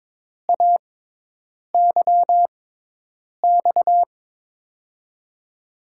Morse code, heard as a keyed tone, about 22 wpm, spaced as for 8 wpm, 700 Hz. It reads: AYX
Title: USA Call Signs - 6 Characters - 6x Character Spacing - 22wpm